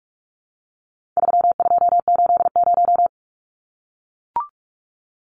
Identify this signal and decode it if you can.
Morse code, heard as a keyed tone, about 44 wpm, 700 Hz. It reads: 3280
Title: Numbers - 4-Digit 45wpm